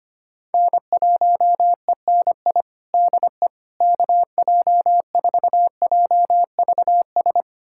Morse code, heard as a keyed tone, about 25 wpm, 700 Hz. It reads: N1ENI DE KJ4JVS
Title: USA Callsign DE Callsign 25wpm